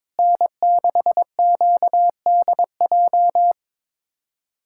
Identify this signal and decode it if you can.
Morse code, heard as a keyed tone, about 22 wpm, 700 Hz. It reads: N6QDJ
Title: USA Callsign DE Callsign 22wpm